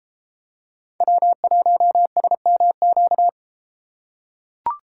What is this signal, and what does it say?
Morse code, heard as a keyed tone, about 33 wpm, 700 Hz. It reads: W1SMQ